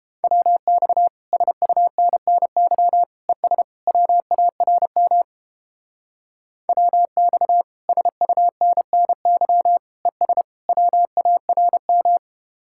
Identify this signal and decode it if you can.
Morse code, heard as a keyed tone, about 29 wpm, 700 Hz. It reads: WX SUNNY ES WARM WX SUNNY ES WARM